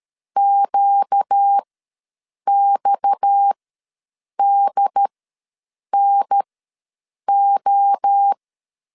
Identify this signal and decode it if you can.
Morse code, heard as a keyed tone, about 13 wpm, spaced as for 7 wpm, 800 Hz. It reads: QXDNO
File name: LEZIONI / 6 PKX0 / 6 PKX ws 12wpm 3106